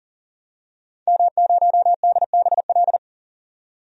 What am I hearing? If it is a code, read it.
Morse code, heard as a keyed tone, about 40 wpm, 700 Hz. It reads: M0DBL
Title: DXCC-100 Call Signs - 5 Characters - 40wpm